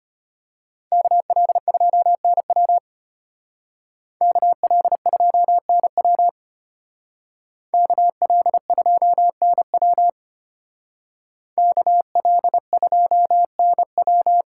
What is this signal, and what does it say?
Morse code, heard as a keyed tone, about 31 wpm, 700 Hz. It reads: KL2NW KL2NW KL2NW KL2NW